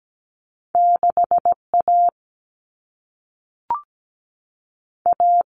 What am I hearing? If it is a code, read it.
Morse code, heard as a keyed tone, about 17 wpm, 700 Hz. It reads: 6A A
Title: Common Cut Numbers - 2-Digit 17wpm